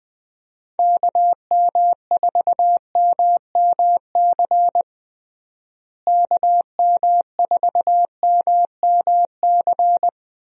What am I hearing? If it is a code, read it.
Morse code, heard as a keyed tone, about 20 wpm, 700 Hz. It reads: KM4MMC KM4MMC
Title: USA Call Signs - 6 Characters - Repeated 2x - 20wpm